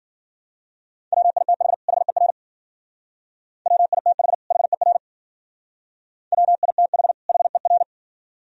Morse code, heard as a keyed tone, about 52 wpm, 700 Hz. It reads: WITH HER WITH HER WITH HER